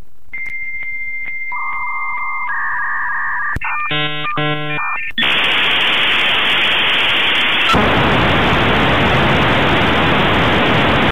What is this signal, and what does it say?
Dial-up modem handshake.
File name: fax-machine-1